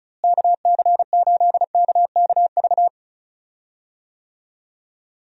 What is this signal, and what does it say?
Morse code, heard as a keyed tone, about 34 wpm, 700 Hz. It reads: KC8KKV
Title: USA Call Signs - 6 Characters - Repeated 2x - 35wpm